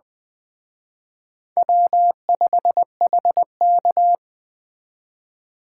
Morse code, heard as a keyed tone, about 21 wpm, 700 Hz. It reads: W5HK